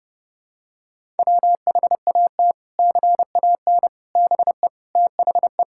Morse code, heard as a keyed tone, about 30 wpm, 700 Hz. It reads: WHAT CAN BE THE
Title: Sentences from Top 1000 Words - Review Part 6 - 30wpm